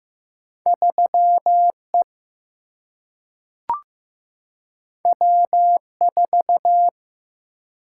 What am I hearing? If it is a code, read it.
Morse code, heard as a keyed tone, about 15 wpm, 700 Hz. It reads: 3 E W 4
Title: Single-Letter Single-Number 15wpm